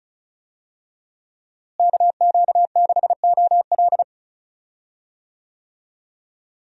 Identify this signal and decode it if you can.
Morse code, heard as a keyed tone, about 36 wpm, 700 Hz. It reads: KQ6OL
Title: USA Call Signs - 5 Characters - Repeated 3x - 35wpm